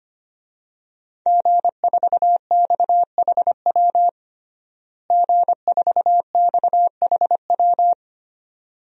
Morse code, heard as a keyed tone, about 25 wpm, 700 Hz. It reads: G4XHW G4XHW
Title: DXCC-100 Call Signs - 5 Characters - Repeated 2x - 25wpm